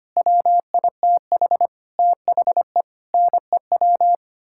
Morse code, heard as a keyed tone, about 25 wpm, 700 Hz. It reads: WITH THE NEW
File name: Sets of 9 Words - 25wpm